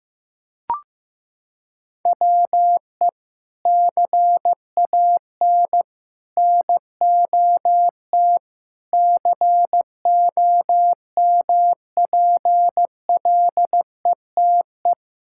Morse code, heard as a keyed tone, about 15 wpm, 700 Hz. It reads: WE CAN NOT COMPLETE
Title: Sentences from Top 1000 Words - Review Part 6 - 15wpm